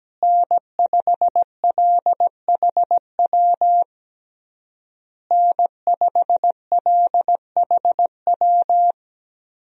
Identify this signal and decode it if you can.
Morse code, heard as a keyed tone, about 17 wpm, 700 Hz. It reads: N5LHW N5LHW
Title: USA Call Signs - 5 Characters - Repeated 3x - 17wpm